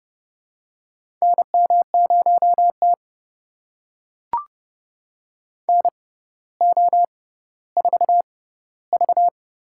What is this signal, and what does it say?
Morse code, heard as a keyed tone, about 31 wpm, spaced as for 15 wpm, 700 Hz. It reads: NM0T NO4V
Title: USA Call Signs - 4 Characters - 6x Character Spacing - 30wpm